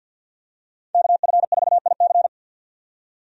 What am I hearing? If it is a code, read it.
Morse code, heard as a keyed tone, about 50 wpm, 700 Hz. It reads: KF4IC
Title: USA Call Signs - 5 Characters - 50wpm